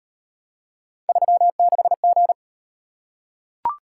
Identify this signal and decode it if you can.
Morse code, heard as a keyed tone, about 38 wpm, 700 Hz. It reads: 36G